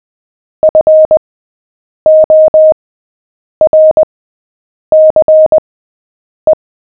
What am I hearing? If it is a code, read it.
Morse code, heard as a keyed tone, about 20 wpm, spaced as for 8 wpm, 620 Hz. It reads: FORCE
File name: cw / fists / Short-Story-7-8wpm